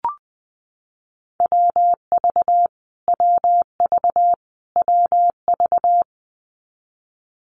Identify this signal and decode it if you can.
Morse code, heard as a keyed tone, about 19 wpm, 700 Hz. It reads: WV WV WV
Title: US States Repeated 3x 20wpm